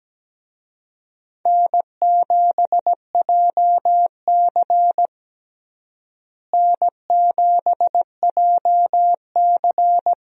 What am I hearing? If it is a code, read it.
Morse code, heard as a keyed tone, about 17 wpm, 700 Hz. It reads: N7JC N7JC
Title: USA Call Signs - 4 Characters - Repeated 2x - 17wpm